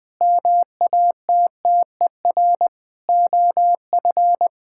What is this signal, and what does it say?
Morse code, heard as a keyed tone, about 20 wpm, 700 Hz. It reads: MATTER OF
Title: Sets of 9 Words - 20wpm